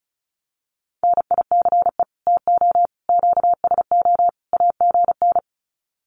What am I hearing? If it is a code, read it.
Morse code, heard as a keyed tone, about 34 wpm, 700 Hz. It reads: NICE TO QSO AGN
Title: Common QSO Phrases 35wpm